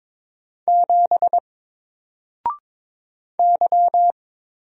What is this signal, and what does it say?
Morse code, heard as a keyed tone, about 22 wpm, 700 Hz. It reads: 7 Y
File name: Single-Letter-Number-22wpm